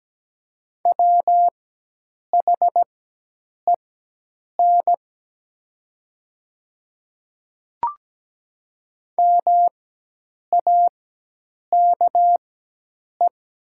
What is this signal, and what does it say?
Morse code, heard as a keyed tone, about 17 wpm, spaced as for 8 wpm, 700 Hz. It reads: WHEN MAKE
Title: 4x Character Spacing - Top 500 Words - 17wpm